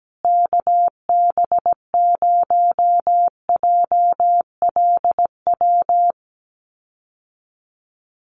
Morse code, heard as a keyed tone, about 17 wpm, 700 Hz. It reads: KB0JLW
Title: USA Call Signs Repeated 2x 17wpm